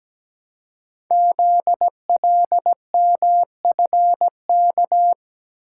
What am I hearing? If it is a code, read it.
Morse code, heard as a keyed tone, about 17 wpm, 700 Hz. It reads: ZLMFK
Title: Code Groups - 5 Letters - 17wpm